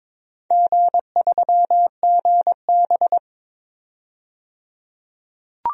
Morse code, heard as a keyed tone, about 22 wpm, 700 Hz. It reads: G3GB
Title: DXCC-100 Call Signs - 4 Characters - 22wpm